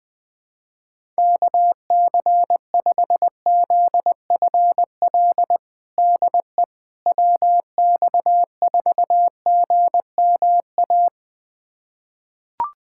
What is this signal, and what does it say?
Morse code, heard as a keyed tone, about 20 wpm, 700 Hz. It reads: KC5ZFL DE WX4GMA
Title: USA Callsign DE Callsign - 6 Characters - 20wpm